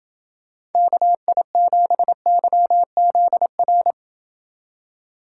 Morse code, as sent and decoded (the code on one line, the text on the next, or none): -.- .. --... -.-- --.. .-.
KI7YZR